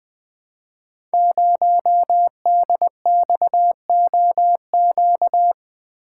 Morse code, heard as a keyed tone, about 20 wpm, 700 Hz. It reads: 0DXOQ
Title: Code Groups - 5 Letter:Numbers - 20wpm